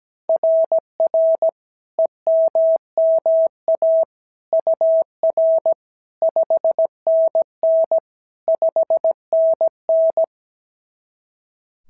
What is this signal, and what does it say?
Morse code, heard as a keyed tone, about 16 wpm, 650 Hz. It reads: RR EMMA UR 5NN 5NN
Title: SOTA QSO Practice 17wpm